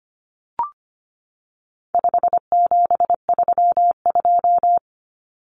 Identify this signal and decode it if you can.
Morse code, heard as a keyed tone, about 25 wpm, 700 Hz. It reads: 5732
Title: Numbers - 4-Digit 25wpm